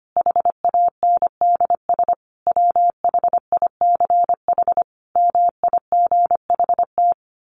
Morse code, heard as a keyed tone, about 25 wpm, 700 Hz. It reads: HANDS WHICH MIGHT
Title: A Sound Of Thunder 25wpm